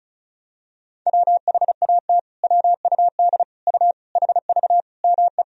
Morse code, heard as a keyed tone, about 36 wpm, 700 Hz. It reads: WHAT WUD U HV ME
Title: Sentences from Top 200 Words and Abbreviations 35wpm